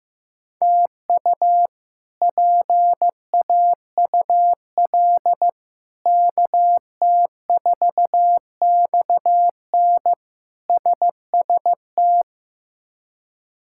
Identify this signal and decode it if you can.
Morse code, heard as a keyed tone, about 15 wpm, 700 Hz. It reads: TU PAUL KT4XN SST
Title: K1USN SST QSO Practice - 15wpm